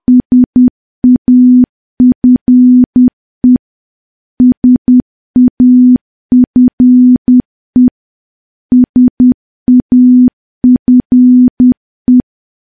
Morse code, heard as a keyed tone, about 10 wpm, 250 Hz. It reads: SAFE SAFE SAFE
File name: morse / morse-training-10WPM